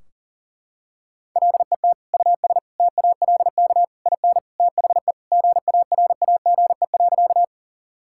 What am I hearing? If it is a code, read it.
Morse code, heard as a keyed tone, about 40 wpm, 700 Hz. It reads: LET US TALK IN THE GARAGE.
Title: Sentences from Top 1600 Words 40wpm